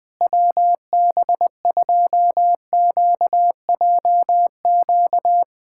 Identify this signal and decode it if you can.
Morse code, heard as a keyed tone, about 20 wpm, 700 Hz. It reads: WB2QJQ
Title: USA Callsign DE Callsign - 6 Characters - 20wpm